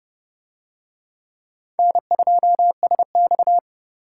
Morse code, heard as a keyed tone, about 31 wpm, 700 Hz. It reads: N2SX